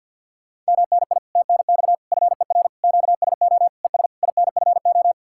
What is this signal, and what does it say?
Morse code, heard as a keyed tone, about 64 wpm, 700 Hz. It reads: MNI TNX FER QSO ES INFO